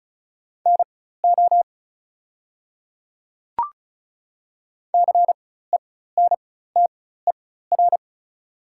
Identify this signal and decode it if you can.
Morse code, heard as a keyed tone, about 36 wpm, spaced as for 16 wpm, 690 Hz. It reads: NO CENTER
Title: 4x Character Spacing - Top 500 Words - 35wpm